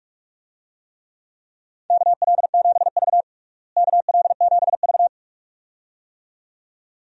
Morse code, heard as a keyed tone, about 47 wpm, 700 Hz. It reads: KL7V KL7V
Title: DXCC-100 Call Signs - 4 Characters - Repeated 2x - 45wpm